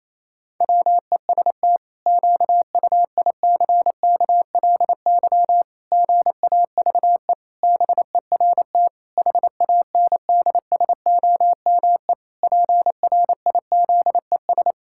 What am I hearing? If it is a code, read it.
Morse code, heard as a keyed tone, about 28 wpm, 700 Hz. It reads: WEST QUICKLY GAVE BERT HANDSOME PRIZES
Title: Pangram Challenge v2 - 28wpm